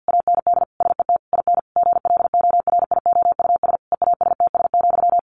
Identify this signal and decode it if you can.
Morse code, heard as a keyed tone, about 50 wpm, 700 Hz. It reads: AND SET IN GLORIOUS EASTS,
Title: A Sound Of Thunder 50wpm